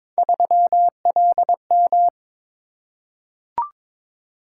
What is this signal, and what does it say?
Morse code, heard as a keyed tone, about 22 wpm, 700 Hz. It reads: 3LM
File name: practice / 22wpm / Three Letter-Number Combinations - 22wpm